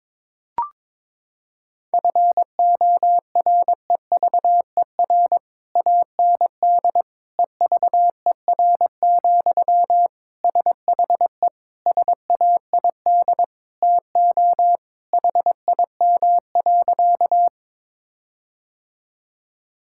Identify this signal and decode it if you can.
Morse code, heard as a keyed tone, about 22 wpm, 700 Hz. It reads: FOREVER AND EVER, SHE SAID TO HIM.
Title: Sentences from Top 1600 Words 22wpm